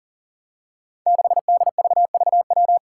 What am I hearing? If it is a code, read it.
Morse code, heard as a keyed tone, about 40 wpm, 700 Hz. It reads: 6DVVW
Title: Five Challenging Letter-Number Combinations 40wpm